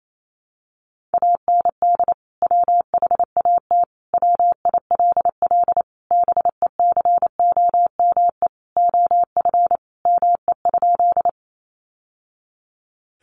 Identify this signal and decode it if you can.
Morse code, heard as a keyed tone, about 28 wpm, 700 Hz. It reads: AND WHAT WILL BECOME OF ME?
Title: Sentences from Top 600 Words 28wpm